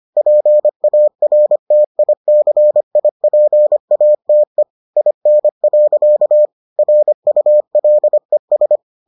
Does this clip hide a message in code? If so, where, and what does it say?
Yes: Morse code from 0.2 to 8.8 s, heard as a keyed tone, about 25 wpm, 600 Hz. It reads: PARTICIPATE IN. RULES